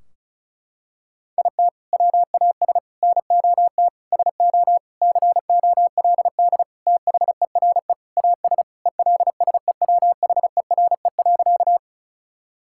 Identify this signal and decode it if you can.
Morse code, heard as a keyed tone, about 34 wpm, 700 Hz. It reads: IT WAS NOT SO COLD THERE AS ELSEWHERE.
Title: Sentences from Top 1600 Words 35wpm